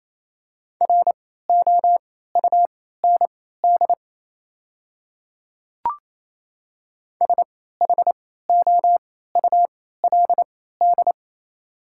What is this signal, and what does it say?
Morse code, heard as a keyed tone, about 28 wpm, spaced as for 16 wpm, 700 Hz. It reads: ROUND SHOULD
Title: 3x Character Spacing - Top 300 Words - 28wpm